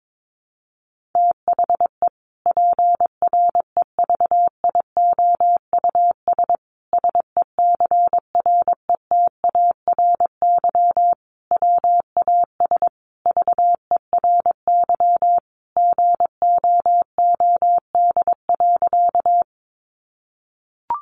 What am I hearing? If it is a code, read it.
Morse code, heard as a keyed tone, about 22 wpm, 700 Hz. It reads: THE PREVIOUS SECRETARY WAS VERY GOOD.
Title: Sentences from Top 1500 Words 22wpm